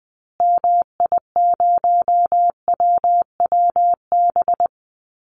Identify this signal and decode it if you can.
Morse code, heard as a keyed tone, about 20 wpm, 700 Hz. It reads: MI0WWB
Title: Call Signs Repeated 2x - 20wpm